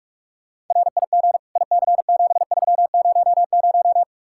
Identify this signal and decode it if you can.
Morse code, heard as a keyed tone, about 34 wpm, 700 Hz. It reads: RIG IC7300
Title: Speed-Racing Common QSO Phrases - 30wpm